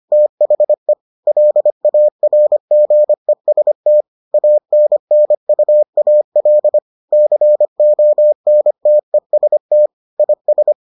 Morse code, heard as a keyed tone, about 25 wpm, 600 Hz. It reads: THE LARGEST ANNUAL CONTEST IS